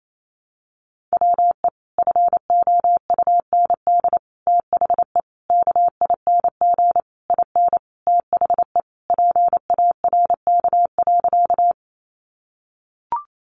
Morse code, heard as a keyed tone, about 28 wpm, 700 Hz. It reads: WE FOUND THE KING IN THE PARK.
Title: Sentences from Top 1300 Words 28wpm